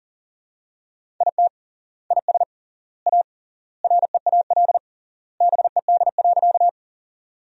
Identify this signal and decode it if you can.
Morse code, heard as a keyed tone, about 40 wpm, 700 Hz. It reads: IT IS A REAL BED.
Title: 3x Word-Spacing - Sentences from Top 500 Words - 40wpm